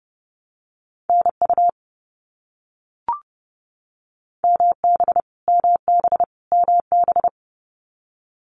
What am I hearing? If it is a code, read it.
Morse code, heard as a keyed tone, about 30 wpm, 700 Hz. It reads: NU MB MB MB
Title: Canadian Provinces and Territories - Repeated 3x - 30wpm